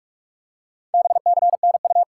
Morse code, heard as a keyed tone, about 44 wpm, 700 Hz. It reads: BCNU
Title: Top 90 QSO Elements 45wpm